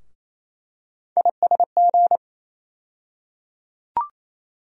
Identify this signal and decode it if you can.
Morse code, heard as a keyed tone, about 28 wpm, 700 Hz. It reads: ISG
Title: USA Call Sign Postfixes - 3 Letters 28wpm